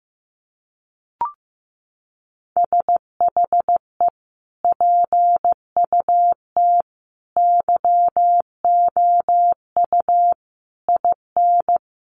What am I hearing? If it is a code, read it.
Morse code, heard as a keyed tone, about 15 wpm, 700 Hz. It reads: SHE PUT YOU IN
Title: Sentences from Top 800 Words 15wpm